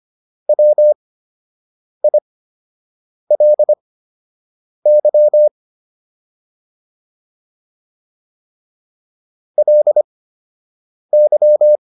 Morse code, heard as a keyed tone, about 25 wpm, spaced as for 7 wpm, 600 Hz. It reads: WILY LY
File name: LY4 07